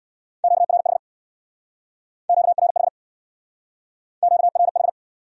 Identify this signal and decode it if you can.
Morse code, heard as a keyed tone, about 63 wpm, 700 Hz. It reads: QRH QRH QRH